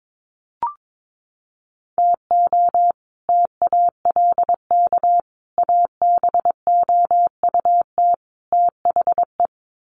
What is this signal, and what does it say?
Morse code, heard as a keyed tone, about 22 wpm, 700 Hz. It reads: TO TALK ABOUT THE
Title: Sets of 4 Words 22wpm